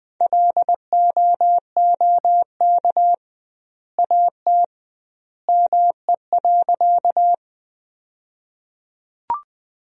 Morse code, heard as a keyed tone, about 20 wpm, 700 Hz. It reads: LOOK AT ME.
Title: 2x Word-Spacing - Sentences from Top 200 Words - 20wpm